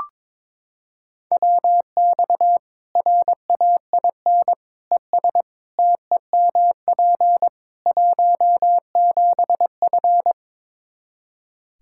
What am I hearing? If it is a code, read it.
Morse code, heard as a keyed tone, about 22 wpm, 700 Hz. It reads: WX RAIN ES TEMP 17F
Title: Practice QSO - Exchange 2 - Info - 22wpm